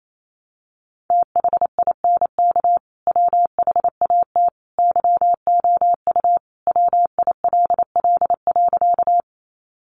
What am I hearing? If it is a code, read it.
Morse code, heard as a keyed tone, about 28 wpm, 700 Hz. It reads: THINK WHAT YOU WILL.
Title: Sentences from Top 200 Words 28wpm